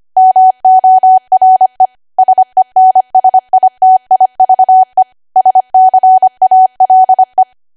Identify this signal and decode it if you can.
Morse code, heard as a keyed tone, about 25 wpm, 750 Hz. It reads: MORE SENSITIVE SCALE